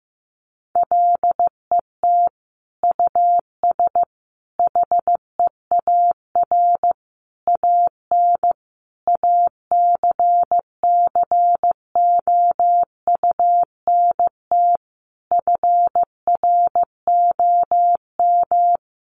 Morse code, heard as a keyed tone, about 15 wpm, 700 Hz. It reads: LET US HEAR AN ACCOUNT FROM
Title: Sentences from Top 900 Words 15wpm